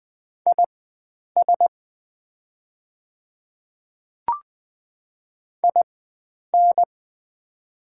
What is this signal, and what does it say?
Morse code, heard as a keyed tone, about 20 wpm, spaced as for 9 wpm, 690 Hz. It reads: IS IN
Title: 4x Character Spacing - Top 100 Words - 20wpm